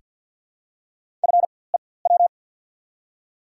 Morse code, heard as a keyed tone, about 48 wpm, spaced as for 23 wpm, 700 Hz. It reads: FEW
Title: 4x Character Spacing - Top 500 Words - 50wpm